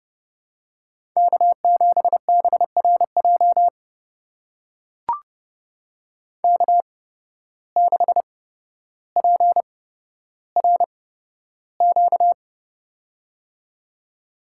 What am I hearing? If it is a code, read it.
Morse code, heard as a keyed tone, about 30 wpm, spaced as for 13 wpm, 700 Hz. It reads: K7BRJ K6PRQ
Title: USA Call Signs - 5 Characters - 8x Character Spacing - 30wpm